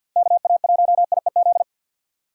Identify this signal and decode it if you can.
Morse code, heard as a keyed tone, about 51 wpm, 690 Hz. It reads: KA1IEZ
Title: USA Call Signs - 4x Character Spacing - 50wpm